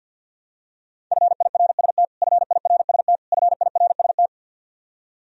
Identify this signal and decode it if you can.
Morse code, heard as a keyed tone, about 52 wpm, 700 Hz. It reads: FIRST FIRST FIRST
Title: Top 100 Words Repeated 50wpm